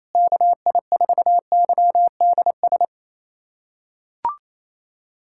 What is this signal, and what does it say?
Morse code, heard as a keyed tone, about 28 wpm, 700 Hz. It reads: KI4YDS
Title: USA Callsign DE Callsign - 6 Characters - 28wpm